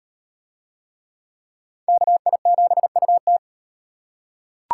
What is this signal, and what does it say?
Morse code, heard as a keyed tone, about 38 wpm, 700 Hz. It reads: KI7UT